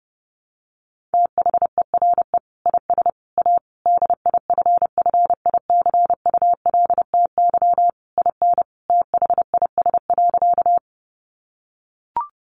Morse code, heard as a keyed tone, about 28 wpm, 700 Hz. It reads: THERE IS A DIFFICULTY IN THIS.
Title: Sentences from Top 1500 Words 30wpm